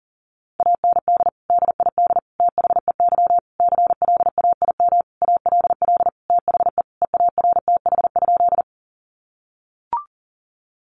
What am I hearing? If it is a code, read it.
Morse code, heard as a keyed tone, about 37 wpm, 700 Hz. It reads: AND DID THEY CLAIM ALL THE EARTH?
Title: Sentences from Top 1500 Words 40wpm